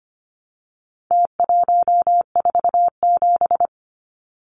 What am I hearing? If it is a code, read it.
Morse code, heard as a keyed tone, about 25 wpm, 700 Hz. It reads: T147
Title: Numbers - Contest Style - 4-Digit 25wpm